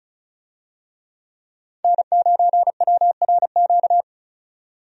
Morse code, heard as a keyed tone, about 36 wpm, 700 Hz. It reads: N9WRQ
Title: USA Call Signs - 5 Characters - 35wpm